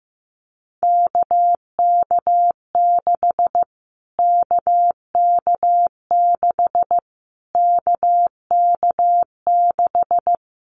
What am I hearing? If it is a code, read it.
Morse code, heard as a keyed tone, about 15 wpm, 700 Hz. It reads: KK6 KK6 KK6
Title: Callsign Prefixes for Top 25 DXCC - Repeated 3x - 15wpm